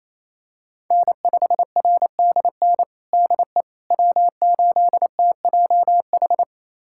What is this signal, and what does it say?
Morse code, heard as a keyed tone, about 28 wpm, 700 Hz. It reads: N5RDN DE W8TJH
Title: USA Callsign DE Callsign - 5 Characters - 28wpm